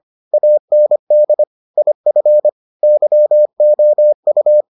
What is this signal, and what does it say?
Morse code, heard as a keyed tone, about 25 wpm, 600 Hz. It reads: AND IF YOU